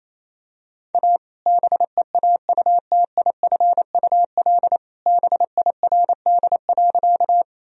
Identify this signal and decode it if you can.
Morse code, heard as a keyed tone, about 27 wpm, 700 Hz. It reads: A BEAUTIFUL BIRD.
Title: Sentences from Top 1000 Words - Review Part 2 - 28wpm